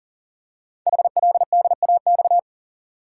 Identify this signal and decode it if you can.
Morse code, heard as a keyed tone, about 42 wpm, 690 Hz. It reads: HLDAX
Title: Code Groups - 5 Letters - 40wpm